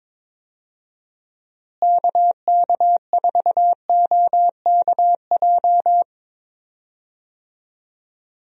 Morse code, heard as a keyed tone, about 22 wpm, 700 Hz. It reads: KK4OKJ
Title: USA Call Signs - 6 Characters - Repeated 2x - 22wpm